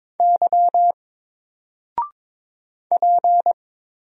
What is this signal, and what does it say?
Morse code, heard as a keyed tone, about 22 wpm, 700 Hz. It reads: Y P
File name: Single-Letter-Number-22wpm